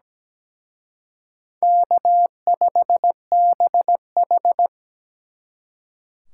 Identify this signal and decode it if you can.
Morse code, heard as a keyed tone, about 20 wpm, 700 Hz. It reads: K5BH
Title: Speed Racing - USA Call Signs 17wpm